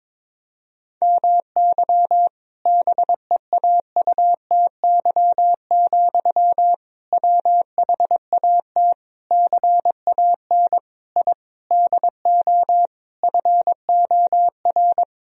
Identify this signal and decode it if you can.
Morse code, heard as a keyed tone, about 22 wpm, 700 Hz. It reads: MY BEAUTY, WHAT CAN I DO FOR
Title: Sentences from Top 1000 Words - Review Part 2 - 22wpm